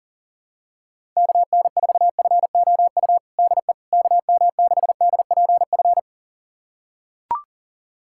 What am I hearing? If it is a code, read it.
Morse code, heard as a keyed tone, about 40 wpm, 700 Hz. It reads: KN4FOU DE KM6DPF
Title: USA Callsign DE Callsign - 6 Characters - 40wpm